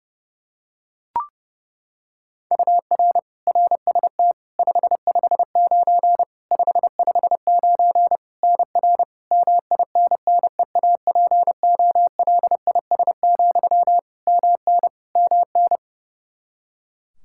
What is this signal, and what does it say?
Morse code, heard as a keyed tone, about 32 wpm, 700 Hz. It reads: UR RST 559 559 NR MINNEAPOLIS, MN MN
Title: Practice QSO - Exchange 1 - Beginning - 30wpm